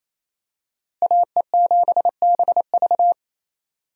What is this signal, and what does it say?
Morse code, heard as a keyed tone, about 28 wpm, 700 Hz. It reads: AE7BV
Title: USA Call Signs - 5 Characters - Repeated 3x - 28wpm